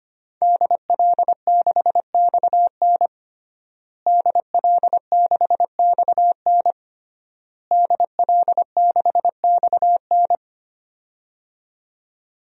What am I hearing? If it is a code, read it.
Morse code, heard as a keyed tone, about 25 wpm, 700 Hz. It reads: DL6XN DL6XN DL6XN
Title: DXCC-100 Call Signs - 5 Characters - Repeated 3x - 25wpm